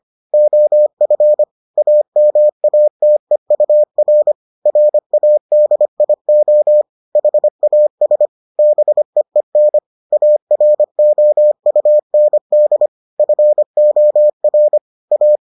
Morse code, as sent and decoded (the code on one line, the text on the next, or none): --- ..-. / .- -- .- - . ..- .-. / .-. .- -.. .. --- / .... .- ... / -... . -. / .- .-. --- ..- -. -.. / ..-. --- .-. / .-
OF AMATEUR RADIO HAS BEEN AROUND FOR A